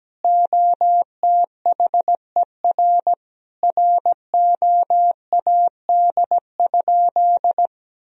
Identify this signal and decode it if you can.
Morse code, heard as a keyed tone, about 17 wpm, 700 Hz. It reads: OTHER ROAD?
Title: Sentences from Top 1000 Words - Review Part 2 - 17wpm